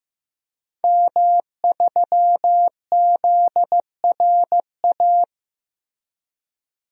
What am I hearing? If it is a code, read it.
Morse code, heard as a keyed tone, about 15 wpm, 700 Hz. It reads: M3ZRA
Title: DXCC-100 Call Signs - 5 Characters - Repeated 2x - 15wpm